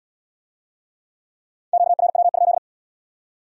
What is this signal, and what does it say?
Morse code, heard as a keyed tone, about 73 wpm, 700 Hz. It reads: QRW?